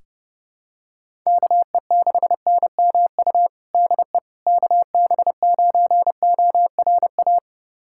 Morse code, heard as a keyed tone, about 30 wpm, 700 Hz. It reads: KE6NMU DE KB9ORA
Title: USA Callsign DE Callsign 30wpm